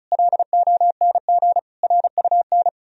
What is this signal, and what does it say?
Morse code, heard as a keyed tone, about 44 wpm, 700 Hz. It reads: LONG RUN